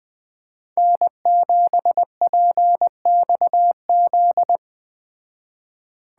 Morse code, heard as a keyed tone, about 21 wpm, 700 Hz. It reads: N7PXZ